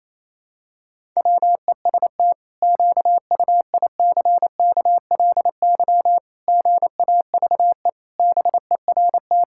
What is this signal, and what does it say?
Morse code, heard as a keyed tone, about 28 wpm, 700 Hz. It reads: WEST QUICKLY GAVE BERT
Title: Pangram Challenge v2 - 28wpm